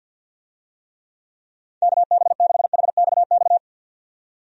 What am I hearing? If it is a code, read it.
Morse code, heard as a keyed tone, about 49 wpm, 700 Hz. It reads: KB6HXX
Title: USA Call Signs - 6 Characters - 50wpm